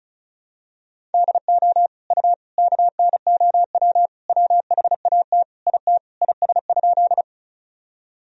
Morse code, heard as a keyed tone, about 36 wpm, 700 Hz. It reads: DO U KNOW WHAT IT IS?
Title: Sentences from Top 100 Words and Abbreviations 35wpm